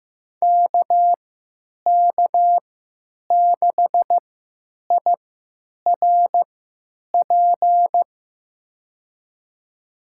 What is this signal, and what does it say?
Morse code, heard as a keyed tone, about 15 wpm, spaced as for 9 wpm, 700 Hz. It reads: KK6IRP
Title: USA Call Signs - 3x Character Spacing - 15wpm